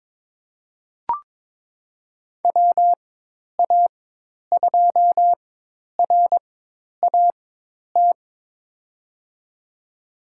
Morse code, heard as a keyed tone, about 22 wpm, spaced as for 10 wpm, 700 Hz. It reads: WA2RAT
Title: USA Call Signs - 4x Character Spacing - 22wpm